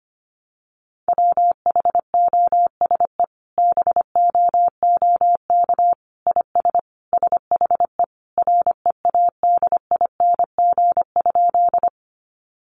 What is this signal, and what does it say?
Morse code, heard as a keyed tone, about 25 wpm, 700 Hz. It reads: WHOSE BOOK IS SHE READING?
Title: Sentences from Top 1200 Words 25wpm